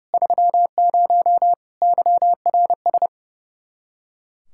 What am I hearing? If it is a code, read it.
Morse code, heard as a keyed tone, about 36 wpm, 700 Hz. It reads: 30 YRS
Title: Speed-Racing Common QSO Phrases - 30wpm